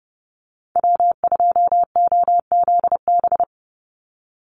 Morse code, heard as a keyed tone, about 30 wpm, 700 Hz. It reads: W2OZB
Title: Call Signs Repeated 3x - 30wpm